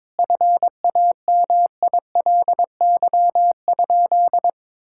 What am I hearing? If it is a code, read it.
Morse code, heard as a keyed tone, about 22 wpm, 700 Hz. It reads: FAMILY?
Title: Sentences from Top 1000 Words - Review Part 6 - 22wpm